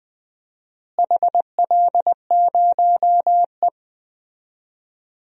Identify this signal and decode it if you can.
Morse code, heard as a keyed tone, about 20 wpm, 700 Hz. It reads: HL0E
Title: DXCC-100 Call Signs - 4 Characters - Repeated 2x - 20wpm